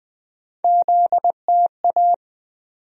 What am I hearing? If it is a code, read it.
Morse code, heard as a keyed tone, about 20 wpm, 700 Hz. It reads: ZTA
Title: USA Call Sign Postfixes - 3 Letters 20wpm